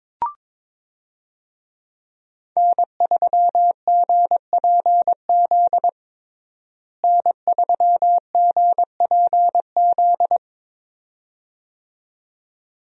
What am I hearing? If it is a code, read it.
Morse code, heard as a keyed tone, about 22 wpm, 700 Hz. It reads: N3GPZ N3GPZ
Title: USA Call Signs - 5 Characters - Repeated 2x - 22wpm